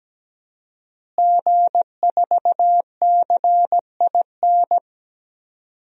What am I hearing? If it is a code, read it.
Morse code, heard as a keyed tone, about 17 wpm, 700 Hz. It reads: G4CIN
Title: DXCC-100 Call Signs - 5 Characters - Repeated 2x - 17wpm